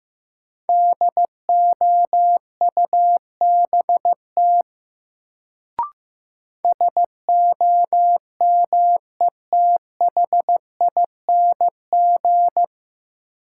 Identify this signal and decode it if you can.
Morse code, heard as a keyed tone, about 15 wpm, 700 Hz. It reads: DOUBT SOMETHING
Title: Sets of 5 Words - New Word Drill - 15wpm